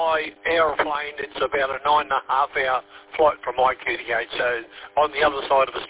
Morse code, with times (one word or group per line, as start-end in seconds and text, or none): none